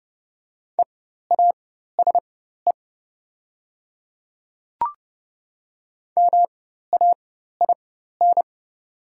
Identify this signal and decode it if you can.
Morse code, heard as a keyed tone, about 30 wpm, spaced as for 14 wpm, 700 Hz. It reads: EASE MAIN
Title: 4x Character Spacing - Top 500 Words - 30wpm